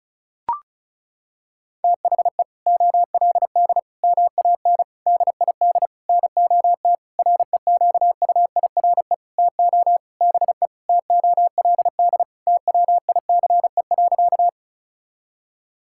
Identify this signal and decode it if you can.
Morse code, heard as a keyed tone, about 36 wpm, 700 Hz. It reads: THE OLD MAN DID NOT REQUIRE TO BE TOLD TWICE.
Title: Sentences from Top 1600 Words 35wpm